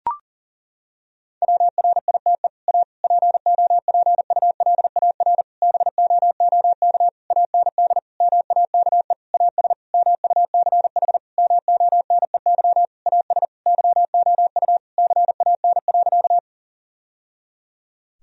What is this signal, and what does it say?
Morse code, heard as a keyed tone, about 40 wpm, 700 Hz. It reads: WRITE A POPULAR BOOK AND MAKE AS MUCH MONEY AS YOU CAN.
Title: Sentences from Top 1000 Words - Review Part 3 - 40wpm